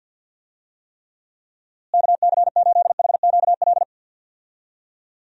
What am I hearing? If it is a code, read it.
Morse code, heard as a keyed tone, about 51 wpm, 700 Hz. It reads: KC8HQL
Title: USA Call Signs - 6 Characters - Repeated 2x - 50wpm